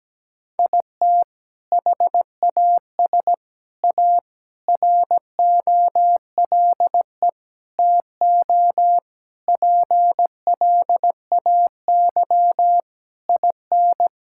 Morse code, heard as a keyed tone, about 17 wpm, 700 Hz. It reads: IT HAS A ROLE TO PLAY IN
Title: Sentences from Top 1000 Words - Review Part 6 - 17wpm